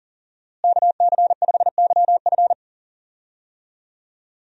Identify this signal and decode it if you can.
Morse code, heard as a keyed tone, about 40 wpm, 700 Hz. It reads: KC5YF
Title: USA Call Signs - 5 Characters - Repeated 2x - 40wpm